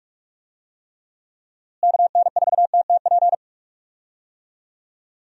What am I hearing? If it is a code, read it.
Morse code, heard as a keyed tone, about 46 wpm, 700 Hz. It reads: KN4TTP
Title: USA Call Signs - 6 Characters - Repeated 2x - 45wpm